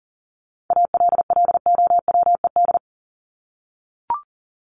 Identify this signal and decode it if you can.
Morse code, heard as a keyed tone, about 40 wpm, 700 Hz. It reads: ALLOWED
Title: ED Suffix - Top 100 Words with Base Words 40wpm